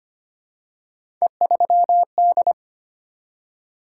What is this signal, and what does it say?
Morse code, heard as a keyed tone, about 25 wpm, 700 Hz. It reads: E3D
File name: Three Letter-Number Combinations - 25wpm